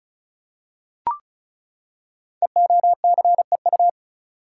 Morse code, heard as a keyed tone, about 36 wpm, 700 Hz. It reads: EOCEU
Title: Code Groups - 5 Letter:Numbers - 35wpm